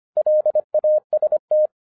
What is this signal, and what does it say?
Morse code, heard as a keyed tone, about 25 wpm, 610 Hz. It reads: LAST